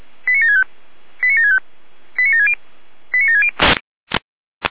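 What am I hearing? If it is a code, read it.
Five-tone selective call (ZVEI), heard as quick runs of five tones, tones of about 80 ms. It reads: 89765 89765 89760 89760
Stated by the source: ZVEI five-tone call 89755, received on SDR.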